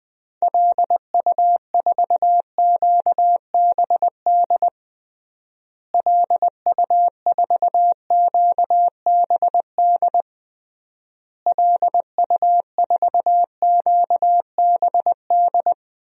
Morse code, heard as a keyed tone, about 20 wpm, 700 Hz. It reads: LU4QBD LU4QBD LU4QBD
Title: DXCC-100 Call Signs - 6 Characters - Repeated 3x - 20wpm